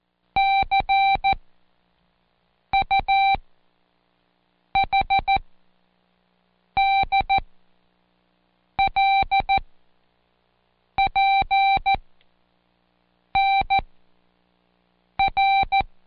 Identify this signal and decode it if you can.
Morse code, heard as a keyed tone, about 14 wpm, spaced as for 5 wpm, 780 Hz. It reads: CUHDLPNR
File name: Code Quick / AudioDownld1 / Seg39-Day5-ReviewAll